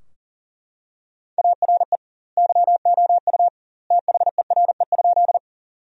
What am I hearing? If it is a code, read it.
Morse code, heard as a keyed tone, about 40 wpm, 700 Hz. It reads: ARE YOU THERE?
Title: 2x Word-Spacing - Sentences from Top 100 Words - 40wpm